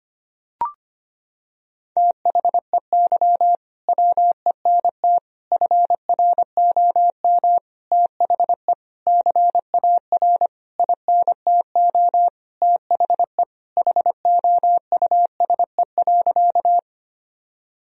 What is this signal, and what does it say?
Morse code, heard as a keyed tone, about 25 wpm, 700 Hz. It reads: THEY WENT FROM THE CAR INTO THE HOUSE.
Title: Sentences from Top 1000 Words - Review Part 1 - 25wpm